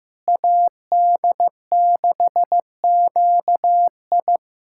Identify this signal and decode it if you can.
Morse code, heard as a keyed tone, about 15 wpm, 700 Hz. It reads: AD6QI
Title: USA Callsign DE Callsign - 5 Characters - 15wpm